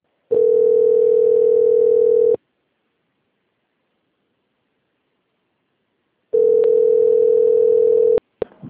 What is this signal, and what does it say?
Ringback tone (North America), 440 + 480 Hz.
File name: out- - - - - .51668